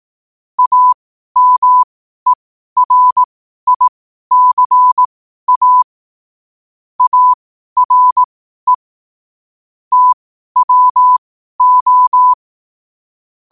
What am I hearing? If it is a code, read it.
Morse code, heard as a keyed tone, about 18 wpm, spaced as for 12 wpm, 1000 Hz. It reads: AMERICA ARE TWO
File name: qotdfiles / 12wpm